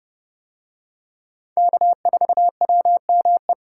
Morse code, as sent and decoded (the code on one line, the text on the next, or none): -.- ....- .-- -- .
K4WME